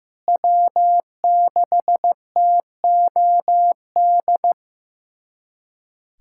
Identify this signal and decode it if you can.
Morse code, heard as a keyed tone, about 15 wpm, 700 Hz. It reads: W6TOD
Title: USA Callsign DE Callsign - 5 Characters - 15wpm